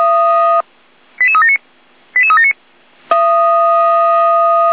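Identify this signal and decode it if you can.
Five-tone selective call (ZVEI), heard as quick runs of five tones, tones of about 70 ms. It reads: 88379 88379
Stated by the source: ZVEI five-tone selective call received on SDR.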